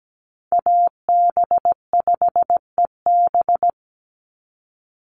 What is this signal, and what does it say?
Morse code, heard as a keyed tone, about 17 wpm, 700 Hz. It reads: AB5EB
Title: Call Signs - 17wpm